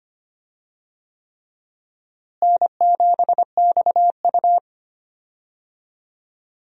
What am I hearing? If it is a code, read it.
Morse code, heard as a keyed tone, about 25 wpm, 700 Hz. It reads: N7XU
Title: USA Call Signs - 4 Characters - Repeated 3x - 25wpm